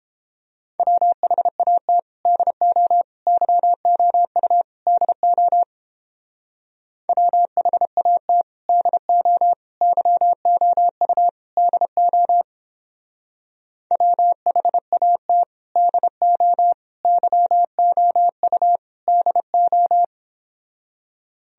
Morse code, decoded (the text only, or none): WHAT DO YOU DO WHAT DO YOU DO WHAT DO YOU DO